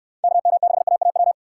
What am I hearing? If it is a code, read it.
Morse code, heard as a keyed tone, about 65 wpm, 700 Hz. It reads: KM6AAW